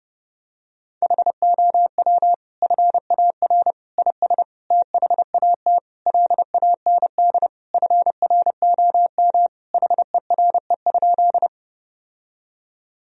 Morse code, heard as a keyed tone, about 30 wpm, 700 Hz. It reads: HOW FAR IS THAT LAND FROM HERE?
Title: Sentences from Top 1000 Words - Review Part 1 - 30wpm